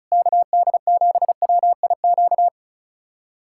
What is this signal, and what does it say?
Morse code, heard as a keyed tone, about 36 wpm, 700 Hz. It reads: KD7WIQ
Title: USA Callsign DE Callsign - 6 Characters - 35wpm